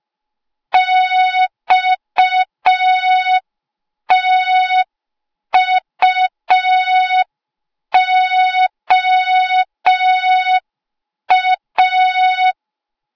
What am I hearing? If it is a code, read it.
Morse code, heard as a keyed tone, about 5 wpm, 3700 Hz. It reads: XTUOA
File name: Morze-HA2MN / hang / 08KXC HA2MN